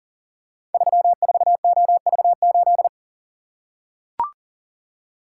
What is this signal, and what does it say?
Morse code, heard as a keyed tone, about 40 wpm, 690 Hz. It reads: 34OV8
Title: Code Groups - 5 Letter:Numbers - 40wpm